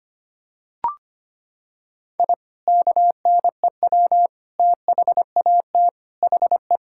Morse code, heard as a keyed tone, about 25 wpm, 700 Hz. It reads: I KNEW THAT HE
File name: practice / 25wpm / Sentences with 2-Word Phrases - 25wpm